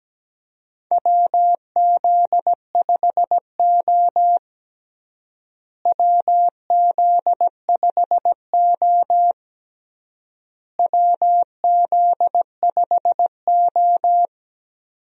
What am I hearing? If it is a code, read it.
Morse code, heard as a keyed tone, about 17 wpm, 700 Hz. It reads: WZ5O WZ5O WZ5O
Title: USA Call Signs - 4 Characters - Repeated 3x - 17wpm